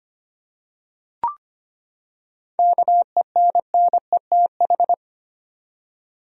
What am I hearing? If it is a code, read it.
Morse code, heard as a keyed tone, about 25 wpm, 700 Hz. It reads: KENNETH
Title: USA Common Names - 25wpm